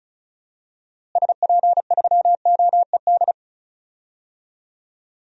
Morse code, heard as a keyed tone, about 34 wpm, 700 Hz. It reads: SP3OED
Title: DXCC-100 Call Signs - 6 Characters - Repeated 3x - 35wpm